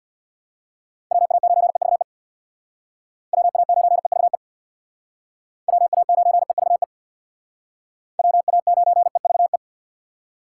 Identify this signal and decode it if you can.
Morse code, heard as a keyed tone, about 65 wpm, 700 Hz. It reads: WA9EVE WA9EVE WA9EVE WA9EVE